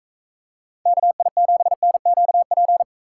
Morse code, heard as a keyed tone, about 45 wpm, 700 Hz. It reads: KI7NQP